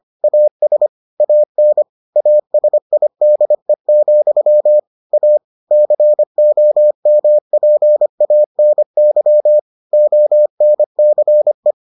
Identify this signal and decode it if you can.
Morse code, heard as a keyed tone, about 25 wpm, 600 Hz. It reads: AS AN ASIDE, A COMPANY ONCE